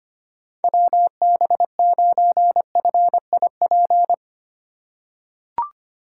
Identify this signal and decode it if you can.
Morse code, heard as a keyed tone, about 25 wpm, 700 Hz. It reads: WB9FIP
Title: K1USN SST QSO Practice - 25wpm